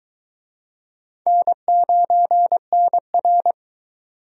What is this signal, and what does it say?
Morse code, heard as a keyed tone, about 23 wpm, 700 Hz. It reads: N9NR